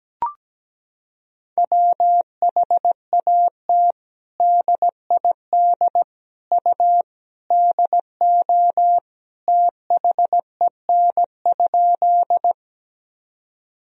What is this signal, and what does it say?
Morse code, heard as a keyed tone, about 17 wpm, 700 Hz. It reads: WHAT DID U DO THEN?
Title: Sentences from Top 100 Words and Abbreviations 17wpm